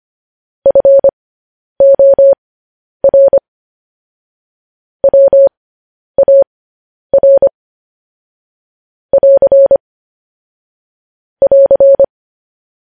Morse code, heard as a keyed tone, about 25 wpm, spaced as for 10 wpm, 550 Hz. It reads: FOR WAR <AR> <AR>
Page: news-headlines-in-morse-code-at-10-wpm